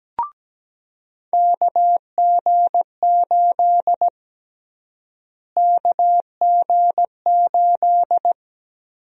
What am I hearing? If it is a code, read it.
Morse code, heard as a keyed tone, about 17 wpm, 700 Hz. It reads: KG8 KG8
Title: USA Call Sign Prefixes - Repeated 2x - 17wpm